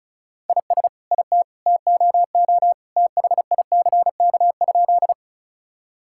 Morse code, heard as a keyed tone, about 36 wpm, 700 Hz. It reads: IS IT TOO THICK?
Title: Sentences from Top 1600 Words 35wpm